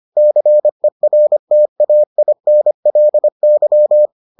Morse code, heard as a keyed tone, about 25 wpm, 600 Hz. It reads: CERTAINLY